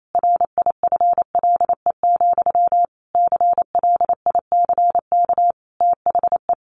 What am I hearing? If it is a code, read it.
Morse code, heard as a keyed tone, about 28 wpm, 700 Hz. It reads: RIFLE, CLICK THE
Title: A Sound Of Thunder 28wpm